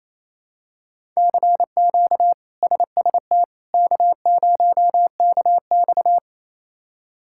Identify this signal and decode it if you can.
Morse code, heard as a keyed tone, about 27 wpm, 700 Hz. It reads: CQ SST K0KX
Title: K1USN SST QSO Practice - 28wpm